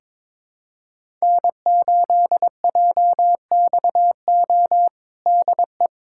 Morse code, heard as a keyed tone, about 22 wpm, 700 Hz. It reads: N8JXO DE
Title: USA Callsign DE Callsign 22wpm